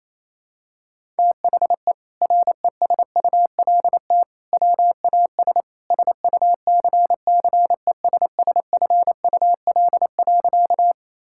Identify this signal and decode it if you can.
Morse code, heard as a keyed tone, about 27 wpm, 700 Hz. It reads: THE RESULT WAS SUCCESSFUL.
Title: Sentences from Top 1000 Words - Review Part 6 - 28wpm